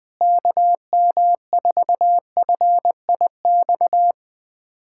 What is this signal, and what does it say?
Morse code, heard as a keyed tone, about 21 wpm, 700 Hz. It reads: KM4FIX
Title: USA Callsign DE Callsign 20wpm